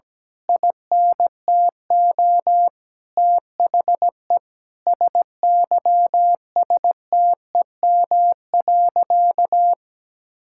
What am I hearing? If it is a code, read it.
Morse code, heard as a keyed tone, about 17 wpm, 700 Hz. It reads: INTO THE SYSTEM.
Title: Sentences from Top 1000 Words - Review Part 6 - 17wpm